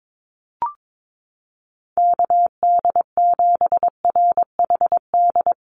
Morse code, heard as a keyed tone, about 22 wpm, 700 Hz. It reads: KD7RHD
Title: USA Call Signs Repeated 3x 22wpm